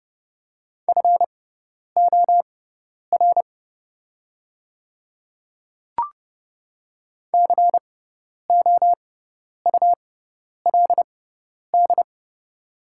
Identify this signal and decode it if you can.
Morse code, heard as a keyed tone, about 30 wpm, spaced as for 10 wpm, 700 Hz. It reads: FOR COULD
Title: 6x Character Spacing - Top 100 Words - 30wpm